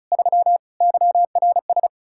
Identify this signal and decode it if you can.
Morse code, heard as a keyed tone, about 43 wpm, 700 Hz. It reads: 3 YRS
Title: Speed-Racing Common QSO Phrases - 35wpm